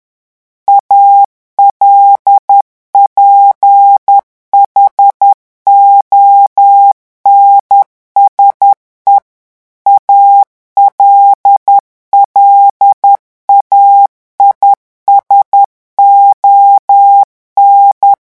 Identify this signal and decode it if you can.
Morse code, heard as a keyed tone, about 10 wpm, 800 Hz. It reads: ALPHONSE ALLAIS ON
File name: revision9 citations humoristiques vitesse 10 mots minute